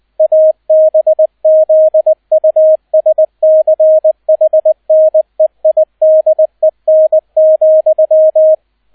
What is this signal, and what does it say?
Morse code, heard as a keyed tone, about 20 wpm, 630 Hz, with radio noise. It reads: ABZUSCHNEIDEN,